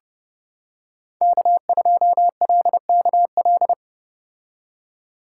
Morse code, heard as a keyed tone, about 30 wpm, 700 Hz. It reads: K2LKL